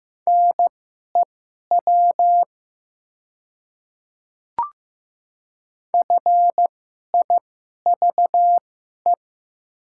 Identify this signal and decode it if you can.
Morse code, heard as a keyed tone, about 15 wpm, spaced as for 11 wpm, 700 Hz. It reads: NEW FIVE
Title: 2x Character Spacing - Top 500 Words - 15wpm